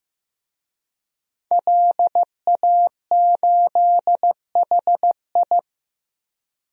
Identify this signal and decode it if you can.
Morse code, heard as a keyed tone, about 15 wpm, 700 Hz. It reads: LA8HI
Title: DXCC-100 Call Signs - 5 Characters - 15wpm